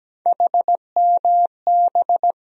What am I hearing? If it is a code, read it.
Morse code, heard as a keyed tone, about 17 wpm, 700 Hz. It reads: HMB
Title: USA Call Sign Postfixes - 3 Letters 17wpm